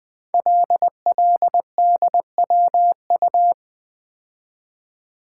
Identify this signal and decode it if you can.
Morse code, heard as a keyed tone, about 20 wpm, 700 Hz. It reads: LLDWU
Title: Code Groups - 5 Letter:Numbers - 20wpm